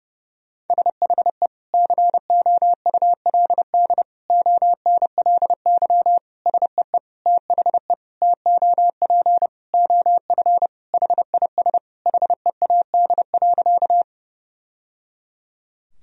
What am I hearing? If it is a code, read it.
Morse code, heard as a keyed tone, about 30 wpm, 700 Hz. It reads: SHE COULD ONLY SEE THE TOP OF HIS HEAD.
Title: Sentences from Top 1000 Words - Review Part 2 - 30wpm